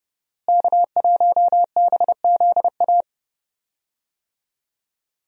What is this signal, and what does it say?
Morse code, heard as a keyed tone, about 31 wpm, 700 Hz. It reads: K1BZA
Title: USA Call Signs - 5 Characters - Repeated 3x - 30wpm